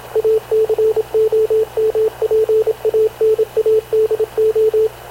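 Morse code, heard as a keyed tone, about 26 wpm, 440 Hz, with radio noise. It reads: ACOMPANADO
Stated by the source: Havana Radio / CLA sending Spanish press (PX) in Morse, recorded by Radio Officer Paul Dumesnil onboard CNAV QUEST/CZDO.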